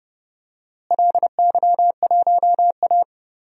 Morse code, heard as a keyed tone, about 30 wpm, 700 Hz. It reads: LY1A
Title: DXCC-100 Call Signs - 4 Characters - Repeated 2x - 30wpm